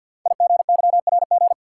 Morse code, heard as a keyed tone, about 49 wpm, 700 Hz. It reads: IGYRG
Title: Code Groups - 5 Letters - 50wpm